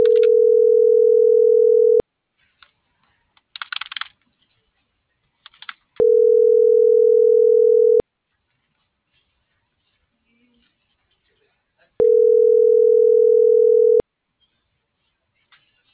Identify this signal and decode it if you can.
Ringback tone (North America), 440 + 480 Hz.
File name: CENTRAL AGUILERA / out brenda.ramos